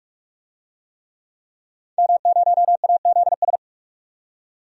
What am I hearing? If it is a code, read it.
Morse code, heard as a keyed tone, about 46 wpm, 700 Hz. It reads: M0AZS
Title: DXCC-100 Call Signs - 5 Characters - 45wpm